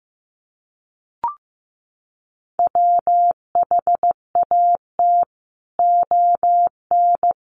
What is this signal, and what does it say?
Morse code, heard as a keyed tone, about 15 wpm, 700 Hz. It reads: WHAT ON
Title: Sentences from Top 200 Words 15wpm